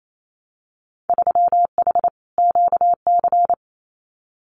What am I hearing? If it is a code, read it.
Morse code, heard as a keyed tone, about 28 wpm, 700 Hz. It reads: 3H QC
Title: Winter Field Day Exchanges 28wpm